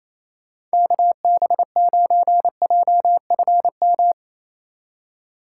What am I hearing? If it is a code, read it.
Morse code, heard as a keyed tone, about 34 wpm, 700 Hz. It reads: KB9JFM
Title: Speed Racing - USA Call Signs 28wpm